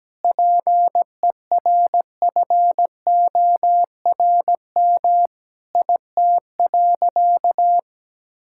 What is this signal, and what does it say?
Morse code, heard as a keyed tone, about 17 wpm, 700 Hz. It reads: PERFORM IT.
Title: Sentences from Top 1000 Words - Review Part 5 - 17wpm